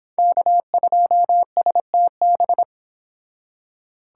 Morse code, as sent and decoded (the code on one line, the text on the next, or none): -.- ..--- ... - -...
K2STB